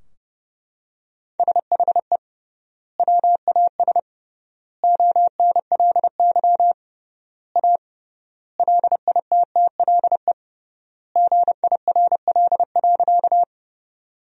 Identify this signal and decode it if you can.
Morse code, heard as a keyed tone, about 30 wpm, 700 Hz. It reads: SHE WAS ONLY A LITTLE GIRL.
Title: 3x Word-Spacing - Sentences from Top 400 Words - 30wpm